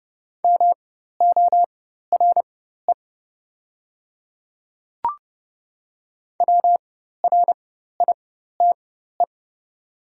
Morse code, heard as a keyed tone, about 30 wpm, spaced as for 14 wpm, 700 Hz. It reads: MORE WRITE
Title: 4x Character Spacing - Top 100 Words - 30wpm